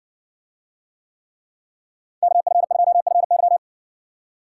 Morse code, heard as a keyed tone, about 60 wpm, 700 Hz. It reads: KF2FY